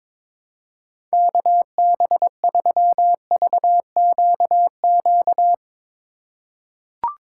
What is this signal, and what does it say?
Morse code, heard as a keyed tone, about 22 wpm, 700 Hz. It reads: KB3VQQ
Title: USA Call Signs - 6 Characters - 4x Character Spacing - 22wpm